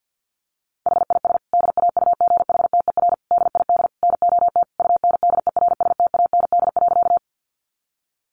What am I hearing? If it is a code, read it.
Morse code, heard as a keyed tone, about 50 wpm, 700 Hz. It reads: HIS DAUGHTER DID NOT UNDERSTAND.
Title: Sentences from Top 1500 Words 50wpm